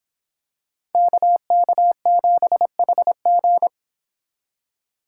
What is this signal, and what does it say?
Morse code, heard as a keyed tone, about 24 wpm, 700 Hz. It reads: KK7HG